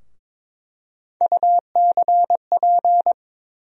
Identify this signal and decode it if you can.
Morse code, heard as a keyed tone, about 22 wpm, 700 Hz. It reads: UCP
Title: USA Call Sign Postfixes - 3 Letters 22wpm